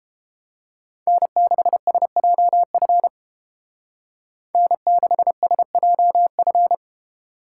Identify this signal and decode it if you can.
Morse code, heard as a keyed tone, about 28 wpm, 700 Hz. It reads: N6SJF N6SJF